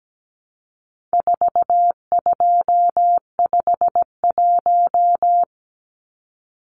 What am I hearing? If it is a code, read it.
Morse code, heard as a keyed tone, about 17 wpm, 700 Hz. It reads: 4251
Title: Numbers - 4-Digit 17wpm